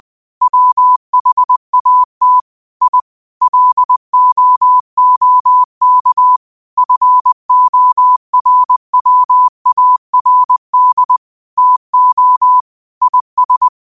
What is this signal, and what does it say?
Morse code, heard as a keyed tone, about 20 wpm, 1000 Hz. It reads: WHAT I LOOK FORWARD TO IS
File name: qotdfiles / 20wpm